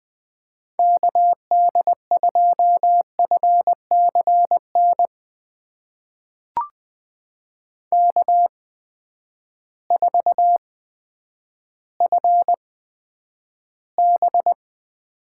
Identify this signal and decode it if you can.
Morse code, heard as a keyed tone, about 20 wpm, spaced as for 10 wpm, 700 Hz. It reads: KD2FCN K4FB
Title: USA Call Signs - 8x Character Spacing - 20wpm